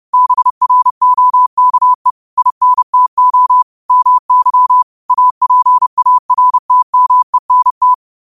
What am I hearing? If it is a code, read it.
Morse code, heard as a keyed tone, about 30 wpm, 1000 Hz. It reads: BROKE INTO MY APARTMENT